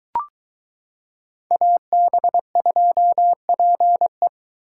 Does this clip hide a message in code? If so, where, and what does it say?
Yes: Morse code from 1.5 to 4.3 s, heard as a keyed tone, about 22 wpm, 700 Hz. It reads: AB2PE